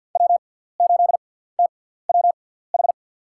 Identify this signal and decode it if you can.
Morse code, heard as a keyed tone, about 53 wpm, spaced as for 17 wpm, 700 Hz. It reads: W8TWH
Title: USA Call Signs - 6x Character Spacing - 50wpm